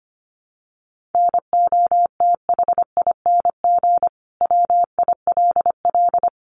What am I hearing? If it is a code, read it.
Morse code, heard as a keyed tone, about 25 wpm, 700 Hz. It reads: NOTHING WILL
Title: Sentences from Top 1400 Words 25wpm